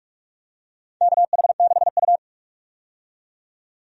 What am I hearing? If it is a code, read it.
Morse code, heard as a keyed tone, about 46 wpm, 700 Hz. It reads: KH6U
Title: DXCC-100 Call Signs - 4 Characters - 45wpm